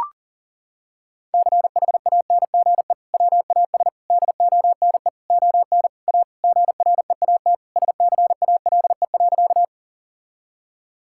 Morse code, heard as a keyed tone, about 40 wpm, 700 Hz. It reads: CHANGE WAS DONE ON A GREAT SCALE.
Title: Sentences from Top 1000 Words - Review Part 5 - 40wpm